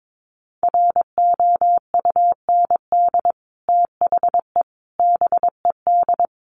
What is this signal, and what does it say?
Morse code, heard as a keyed tone, about 22 wpm, 700 Hz. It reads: ROUND THE BED
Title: Sentences from Top 1000 Words 22wpm